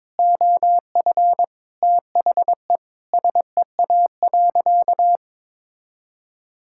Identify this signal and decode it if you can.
Morse code, heard as a keyed tone, about 22 wpm, 700 Hz. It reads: OF THE SEA.
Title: Sentences from Top 1000 Words - Review Part 5 - 22wpm